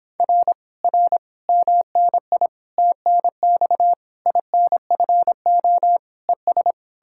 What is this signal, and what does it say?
Morse code, heard as a keyed tone, about 29 wpm, 700 Hz. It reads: R R MNI TNX INFO ES